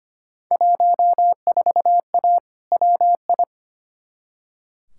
Morse code, heard as a keyed tone, about 25 wpm, 700 Hz. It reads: 14A WI
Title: Field Day Exchanges Encore - 25wpm